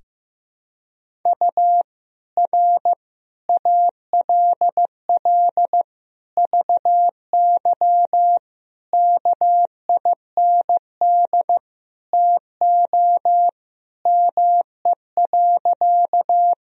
Morse code, heard as a keyed tone, about 15 wpm, 700 Hz. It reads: U R ALL VY KIND TO ME.
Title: Sentences from Top 200 Words and Abbreviations 15wpm